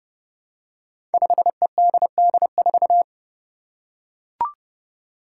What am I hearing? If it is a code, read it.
Morse code, heard as a keyed tone, about 30 wpm, 700 Hz. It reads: 5EDD4
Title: Five Challenging Letter-Number Combinations 30wpm